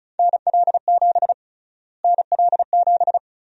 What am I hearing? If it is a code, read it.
Morse code, heard as a keyed tone, about 36 wpm, 700 Hz. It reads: NL7 NL7
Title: USA Call Sign Prefixes - Repeated 2x - 35wpm